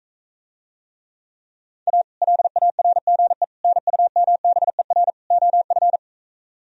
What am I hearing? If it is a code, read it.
Morse code, heard as a keyed tone, about 45 wpm, 700 Hz. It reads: A LARGE NUMBER OF